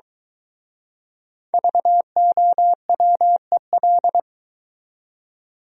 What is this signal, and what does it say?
Morse code, heard as a keyed tone, about 23 wpm, 700 Hz. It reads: VOWEL